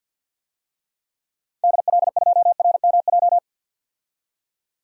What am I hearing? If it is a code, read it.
Morse code, heard as a keyed tone, about 46 wpm, 700 Hz. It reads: DL2RMJ
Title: DXCC-100 Call Signs - 6 Characters - 50wpm